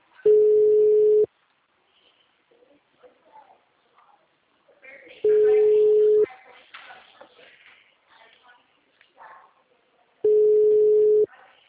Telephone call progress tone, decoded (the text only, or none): ringback tone